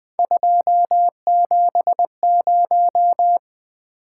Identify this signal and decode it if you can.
Morse code, heard as a keyed tone, about 24 wpm, 700 Hz. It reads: 270
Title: Speed-Racing - Three Letter-Number Combinations - 20wpm